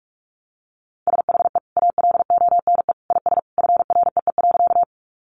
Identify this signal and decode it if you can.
Morse code, heard as a keyed tone, about 44 wpm, 700 Hz. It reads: SHE ALONE IS FREE.
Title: Sentences from Top 1000 Words 45wpm